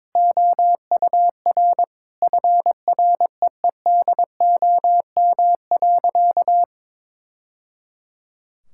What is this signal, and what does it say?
Morse code, heard as a keyed tone, about 22 wpm, 700 Hz. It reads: OUR FREEDOM.
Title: Sentences from Top 1000 Words - Review Part 5 - 22wpm